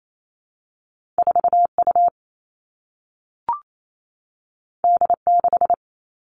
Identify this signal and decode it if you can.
Morse code, heard as a keyed tone, about 28 wpm, 700 Hz. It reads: 4 U D 6
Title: Two Challenging Letter-Number Combinations 28wpm